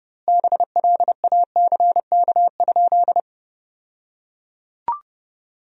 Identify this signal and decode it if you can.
Morse code, heard as a keyed tone, about 32 wpm, 700 Hz. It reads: BLACK?
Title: 2x Word-Spacing - Sentences from Top 400 Words - 30wpm